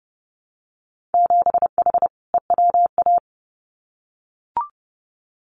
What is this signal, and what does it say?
Morse code, heard as a keyed tone, about 30 wpm, 700 Hz. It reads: 7H EWA
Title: Winter Field Day Exchanges Encore 30wpm